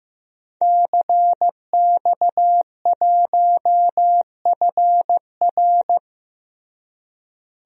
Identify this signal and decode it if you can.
Morse code, heard as a keyed tone, about 15 wpm, 700 Hz. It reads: CX1FR
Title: DXCC-100 Call Signs - 5 Characters - Repeated 2x - 15wpm